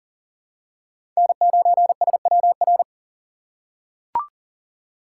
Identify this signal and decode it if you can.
Morse code, heard as a keyed tone, about 40 wpm, 700 Hz. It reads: N9SWR
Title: USA Call Signs - 5 Characters - 3x Character Spacing - 40wpm